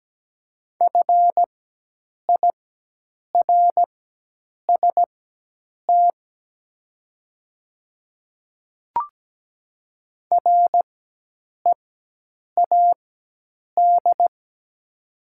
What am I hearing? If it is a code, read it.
Morse code, heard as a keyed tone, about 17 wpm, spaced as for 8 wpm, 700 Hz. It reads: FIRST READ
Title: 4x Character Spacing - Top 200 Words - 17wpm